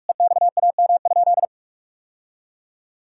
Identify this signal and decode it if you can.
Morse code, heard as a keyed tone, about 46 wpm, 700 Hz. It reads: EXAM?
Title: Sentences from Top 1000 Words - Review Part 5 - 45wpm